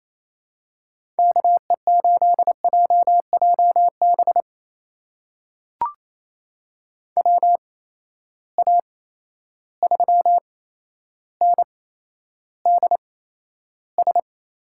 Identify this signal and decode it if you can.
Morse code, heard as a keyed tone, about 28 wpm, spaced as for 12 wpm, 700 Hz. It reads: KE8JJB WA3NDS
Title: USA Call Signs - 6 Characters - 8x Character Spacing - 28wpm